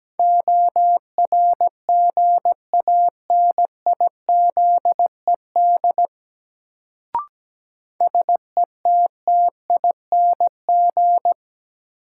Morse code, heard as a keyed tone, about 17 wpm, 700 Hz. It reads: ORGANIZED SETTING
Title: Sentences from Top 1300 Words - New Word Drill - 17wpm